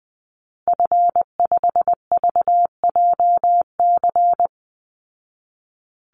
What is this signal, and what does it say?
Morse code, heard as a keyed tone, about 20 wpm, 700 Hz. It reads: F5VJC
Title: Call Signs Repeated 2x - 20wpm